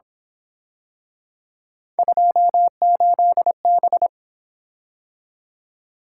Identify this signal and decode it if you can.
Morse code, heard as a keyed tone, about 28 wpm, 700 Hz. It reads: 28B